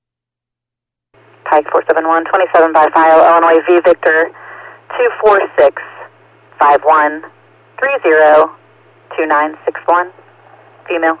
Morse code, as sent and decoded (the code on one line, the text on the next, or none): none